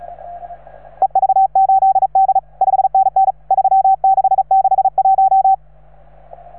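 Morse code, heard as a keyed tone, about 35 wpm, 770 Hz. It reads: EV8D 5NN 3661
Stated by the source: Morse from station 3V9A.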